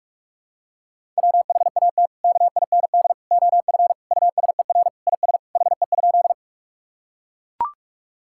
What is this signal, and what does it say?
Morse code, heard as a keyed tone, about 41 wpm, 700 Hz. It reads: WHAT KIND OF USER IS HE?
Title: Sentences from Top 1000 Words - Review Part 5 - 45wpm